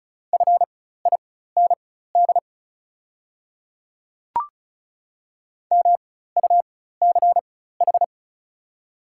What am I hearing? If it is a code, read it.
Morse code, heard as a keyed tone, about 36 wpm, spaced as for 16 wpm, 700 Hz. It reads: FIND MUCH
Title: 4x Character Spacing - Top 200 Words - 35wpm